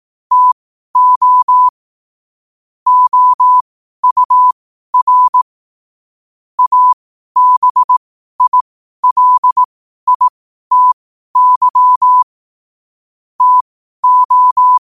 Morse code, heard as a keyed tone, about 18 wpm, spaced as for 12 wpm, 1000 Hz. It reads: TO OUR ABILITY TO
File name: qotdfiles / 12wpm